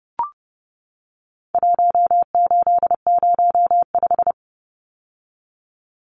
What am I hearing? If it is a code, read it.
Morse code, heard as a keyed tone, about 30 wpm, 700 Hz. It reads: 1805
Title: Numbers - Contest Style - 4-Digit 30wpm